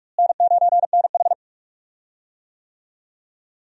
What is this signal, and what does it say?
Morse code, heard as a keyed tone, about 41 wpm, 700 Hz. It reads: N9NH
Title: USA Call Signs - 4 Characters - Repeated 3x - 45wpm